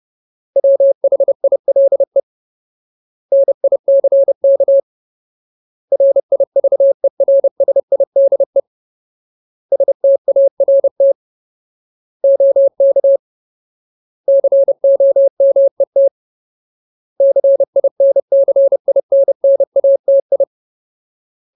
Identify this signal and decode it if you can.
Morse code, heard as a keyed tone, about 30 wpm, 550 Hz. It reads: WHILE NICK RIVERSIDE START OK COMET CINCINNATI
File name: Ultimate Word List / 30WPM Ultimate